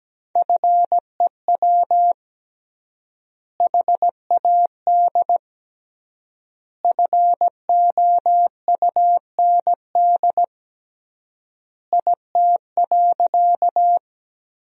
Morse code, heard as a keyed tone, about 17 wpm, 700 Hz. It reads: FEW HAD FOUND IT.
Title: 3x Word-Spacing - Sentences from Top 300 Words - 17wpm